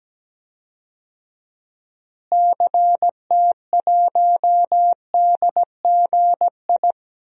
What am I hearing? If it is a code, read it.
Morse code, heard as a keyed tone, about 17 wpm, 700 Hz. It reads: CT1DGI
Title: DXCC-100 Call Signs - 6 Characters - Repeated 2x - 17wpm